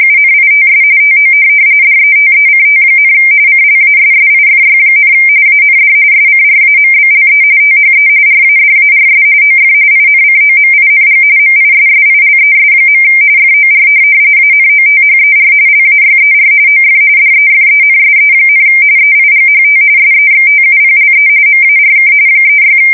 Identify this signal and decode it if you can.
RTTY, about 45.45 baud. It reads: UNITS TO DETECT, CLASSIFY, AND SCRAMBLE THE RADARS OF INCOMING MISSILES. FOR ANOTHER LAYER OF DEFENSE, EIGHT SEXTUPLE MARK 36
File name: fsk170 / USS-New-Jersey